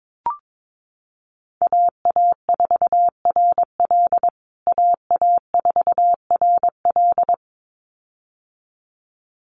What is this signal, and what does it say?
Morse code, heard as a keyed tone, about 22 wpm, 700 Hz. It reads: AA4RL AA4RL
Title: USA Call Signs Repeated 2x 22wpm